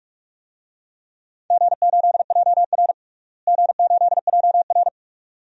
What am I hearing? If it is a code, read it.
Morse code, heard as a keyed tone, about 46 wpm, 700 Hz. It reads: G8JR G8JR
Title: DXCC-100 Call Signs - 4 Characters - Repeated 2x - 45wpm